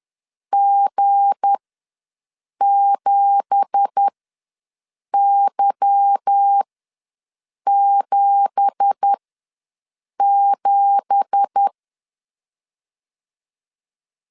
Morse code, heard as a keyed tone, about 10 wpm, spaced as for 6 wpm, 800 Hz. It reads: G7Y77
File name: LEZIONI / 9 2867 ws 10wpm 4215